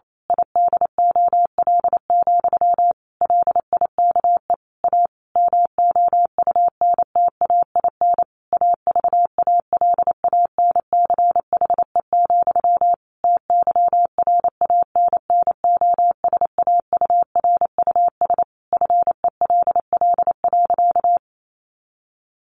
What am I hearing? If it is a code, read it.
Morse code, heard as a keyed tone, about 28 wpm, 700 Hz. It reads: IDOL, LIKE A MOUNTAIN AVALANCHE, TYRANNOSAURUS FELL.
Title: A Sound Of Thunder 28wpm